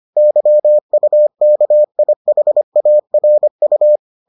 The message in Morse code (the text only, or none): YUKIHARU